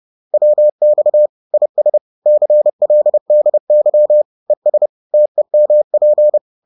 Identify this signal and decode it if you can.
Morse code, heard as a keyed tone, about 30 wpm, 600 Hz. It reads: WX IS CLDY ES TEMP
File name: 30wpm-4